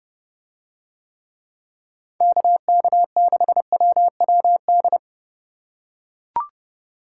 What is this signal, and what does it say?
Morse code, heard as a keyed tone, about 28 wpm, 700 Hz. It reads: KK6WWD